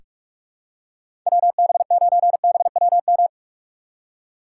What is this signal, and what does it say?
Morse code, heard as a keyed tone, about 46 wpm, 700 Hz. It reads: WB9BWM
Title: USA Call Signs - 6 Characters - 6x Character Spacing - 45wpm